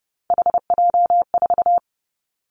Morse code, heard as a keyed tone, about 30 wpm, 700 Hz. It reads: HJ4
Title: Callsign Prefixes for Top 100 DXCC - 30wpm